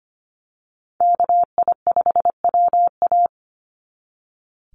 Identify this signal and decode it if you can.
Morse code, heard as a keyed tone, about 25 wpm, 700 Hz. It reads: KI5WA
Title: NA SOTA Chaser Callsigns 25wpm